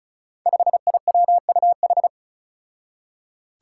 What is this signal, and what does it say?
Morse code, heard as a keyed tone, about 34 wpm, 700 Hz. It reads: 5IWUH
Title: Five Challenging Letter-Number Combinations 35wpm